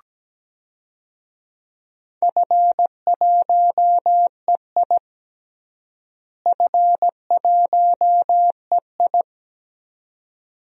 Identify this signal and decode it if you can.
Morse code, heard as a keyed tone, about 17 wpm, 700 Hz. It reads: F1EI F1EI
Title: DXCC-100 Call Signs - 4 Characters - Repeated 2x - 17wpm